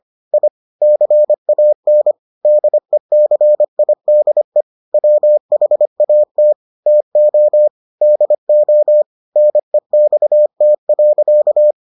Morse code, heard as a keyed tone, about 25 wpm, 600 Hz. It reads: I CAN DECIDE WHAT TO DO NEXT.